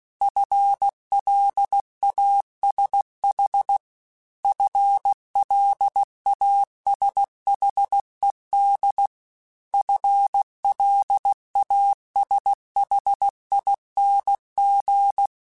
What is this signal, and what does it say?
Morse code, heard as a keyed tone, about 16 wpm, 800 Hz. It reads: FLASH FLASHED FLASHING